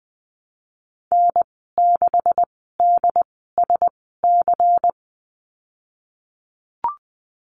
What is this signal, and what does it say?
Morse code, heard as a keyed tone, about 20 wpm, spaced as for 15 wpm, 700 Hz. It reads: N6DSC
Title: 147 Lesson 24g - Letter y - Call Signs 20wpm